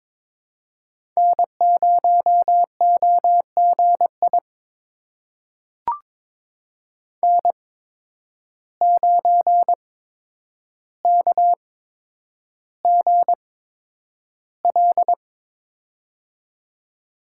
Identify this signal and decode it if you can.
Morse code, heard as a keyed tone, about 22 wpm, spaced as for 9 wpm, 700 Hz. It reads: N0OGI N9KGL
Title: USA Call Signs - 5 Characters - 8x Character Spacing - 22wpm